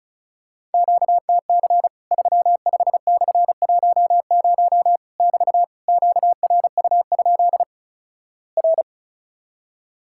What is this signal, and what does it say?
Morse code, heard as a keyed tone, about 36 wpm, 700 Hz. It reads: QTC 35/10 <BT> QRU? R
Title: Worked All Europe DX Contest - QTC Exchange - 35wpm